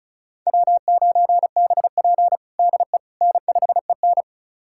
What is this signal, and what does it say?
Morse code, heard as a keyed tone, about 36 wpm, 700 Hz. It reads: W9BP DE N5EN
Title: USA Callsign DE Callsign - 4 Characters - 35wpm